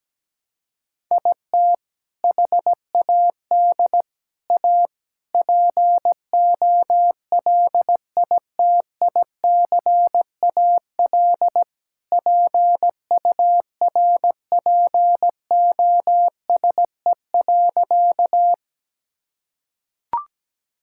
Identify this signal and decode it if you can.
Morse code, heard as a keyed tone, about 17 wpm, 700 Hz. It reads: IT HAD A POLITICAL PURPOSE.
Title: Sentences from Top 1000 Words - Review Part 5 - 17wpm